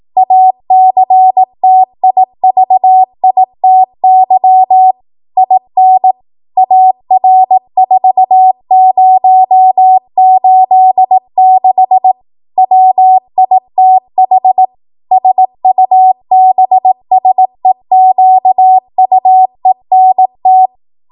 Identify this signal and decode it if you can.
Morse code, heard as a keyed tone, about 18 wpm, 750 Hz. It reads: ACTIVITY IN AR4086 WITH SUBSEQUENT
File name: Morse / 18WPM